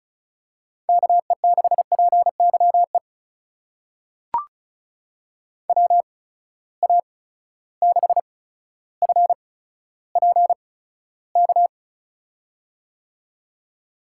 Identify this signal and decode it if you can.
Morse code, heard as a keyed tone, about 36 wpm, spaced as for 15 wpm, 700 Hz. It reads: KE6PYE WA6FPK
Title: USA Call Signs - 6 Characters - 8x Character Spacing - 35wpm